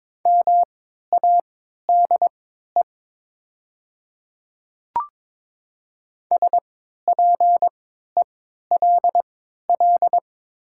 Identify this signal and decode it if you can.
Morse code, heard as a keyed tone, about 22 wpm, spaced as for 12 wpm, 700 Hz. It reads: MADE SPELL
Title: 3x Character Spacing - Top 300 Words - 22wpm